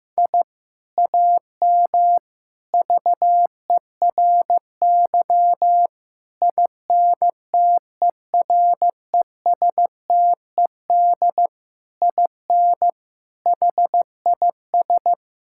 Morse code, heard as a keyed tone, about 16 wpm, 700 Hz. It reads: I AM VERY INTERESTED IN HIS
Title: Sentences from Top 1000 Words - Review Part 6 - 15wpm